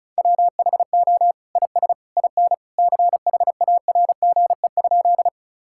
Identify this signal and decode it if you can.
Morse code, heard as a keyed tone, about 36 wpm, 700 Hz. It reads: WHO IS IN CHARGE?
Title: Sentences from Top 1000 Words - Review Part 6 - 35wpm